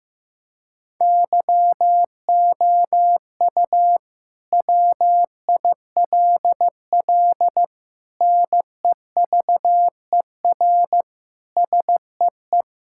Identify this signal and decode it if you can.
Morse code, heard as a keyed tone, about 15 wpm, 700 Hz. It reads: YOU WILL NEVER SEE
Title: Sentences from Top 1600 Words 15wpm